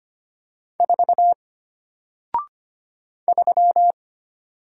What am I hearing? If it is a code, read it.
Morse code, heard as a keyed tone, about 25 wpm, 700 Hz. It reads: 4 3
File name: Single-Letter-Number-25wpm